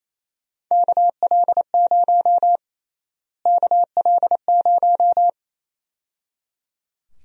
Morse code, heard as a keyed tone, about 28 wpm, 700 Hz. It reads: KL0 KL0
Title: USA Call Sign Prefixes - Repeated 2x - 28wpm